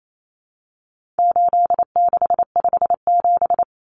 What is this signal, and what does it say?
Morse code, heard as a keyed tone, about 28 wpm, 700 Hz. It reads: 8657
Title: Common Cut Numbers - 4-Digit 28wpm